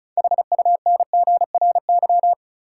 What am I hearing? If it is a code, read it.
Morse code, heard as a keyed tone, about 36 wpm, 700 Hz. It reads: HUNGRY
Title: Sentences from Top 1400 Words - New Word Drill - 35wpm